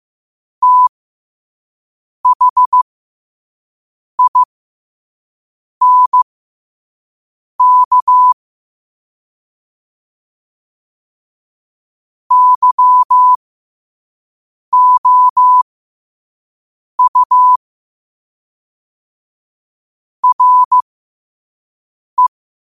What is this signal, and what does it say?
Morse code, heard as a keyed tone, about 15 wpm, spaced as for 5 wpm, 1000 Hz. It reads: THINK YOU RE
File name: qotdfiles / 5wpm